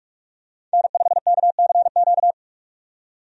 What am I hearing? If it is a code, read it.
Morse code, heard as a keyed tone, about 41 wpm, 700 Hz. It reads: N5KCQ
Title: USA Call Signs - 5 Characters - 45wpm